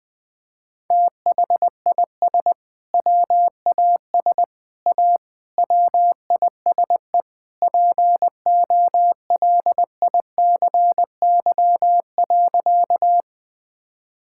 Morse code, heard as a keyed tone, about 21 wpm, 700 Hz. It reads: THIS WAS A WISE POLICY.
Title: Sentences from Top 1600 Words 20wpm